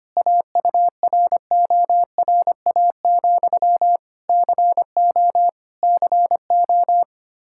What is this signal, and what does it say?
Morse code, heard as a keyed tone, about 25 wpm, 700 Hz. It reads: AURORA, CO CO
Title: Practice QSO - Exchange 1 - Beginning - 25wpm